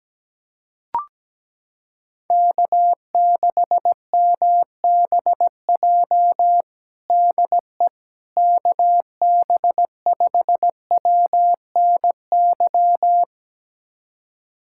Morse code, heard as a keyed tone, about 17 wpm, 700 Hz. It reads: K6MBJ DE KB5WNY
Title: USA Callsign DE Callsign 17wpm